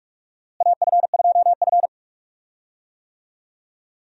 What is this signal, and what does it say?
Morse code, heard as a keyed tone, about 42 wpm, 700 Hz. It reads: AF2F
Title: USA Call Signs - 4 Characters - Repeated 3x - 45wpm